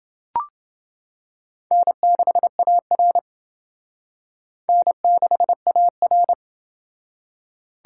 Morse code, heard as a keyed tone, about 28 wpm, 700 Hz. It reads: N6AR N6AR